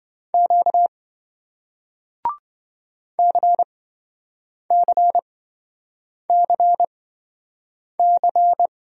Morse code, heard as a keyed tone, about 28 wpm, spaced as for 7 wpm, 700 Hz. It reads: Q CCCC